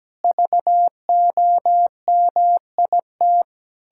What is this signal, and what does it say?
Morse code, heard as a keyed tone, about 17 wpm, 700 Hz. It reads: VOMIT
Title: Pangram Challenge v2 - 17wpm